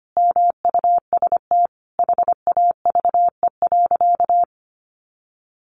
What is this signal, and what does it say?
Morse code, heard as a keyed tone, about 25 wpm, 700 Hz. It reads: MUST HAVE.
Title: Sentences from Top 1100 Words 25wpm